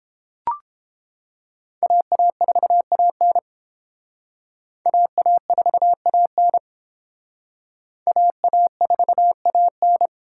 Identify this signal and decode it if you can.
Morse code, heard as a keyed tone, about 29 wpm, 700 Hz. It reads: AA4AN AA4AN AA4AN